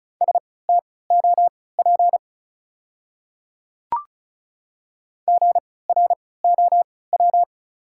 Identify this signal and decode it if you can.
Morse code, heard as a keyed tone, about 34 wpm, spaced as for 20 wpm, 700 Hz. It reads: STOP GROW
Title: 3x Character Spacing - Top 300 Words - 35wpm